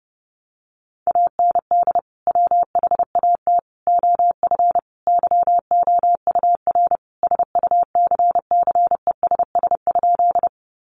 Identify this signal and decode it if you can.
Morse code, heard as a keyed tone, about 30 wpm, 700 Hz. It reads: AND WHAT OF YOUR SUCCESS?
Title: Sentences from Top 900 Words 30wpm